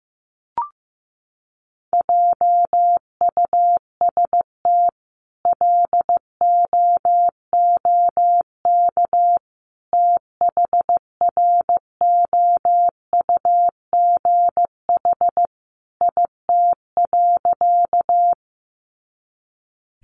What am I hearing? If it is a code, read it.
Morse code, heard as a keyed tone, about 15 wpm, 700 Hz. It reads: JUST LOOK THROUGH IT.
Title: Sentences from Top 200 Words 15wpm